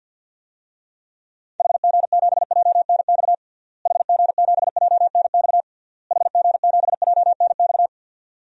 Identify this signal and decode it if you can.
Morse code, heard as a keyed tone, about 52 wpm, 700 Hz. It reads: HG7JNX HG7JNX HG7JNX
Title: DXCC-100 Call Signs - 6 Characters - Repeated 3x - 50wpm